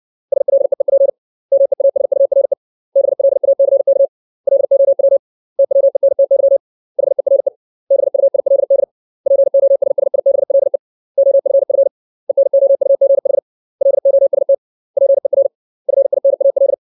Morse code, heard as a keyed tone, about 60 wpm, 550 Hz. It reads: SLEEP MEASURE BLACK LOW TWENTY HERE BUILD POSSIBLE OFF ENOUGH LOST PER FINAL
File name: 632 words / 632 Words 60WPM